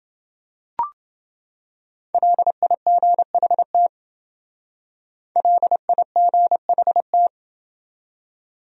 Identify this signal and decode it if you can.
Morse code, heard as a keyed tone, about 29 wpm, 700 Hz. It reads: LIGHT LIGHT